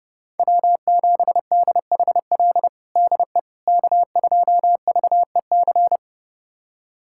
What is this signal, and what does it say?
Morse code, heard as a keyed tone, about 30 wpm, 700 Hz. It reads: W7DHL DE K2VEC
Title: USA Callsign DE Callsign - 5 Characters - 30wpm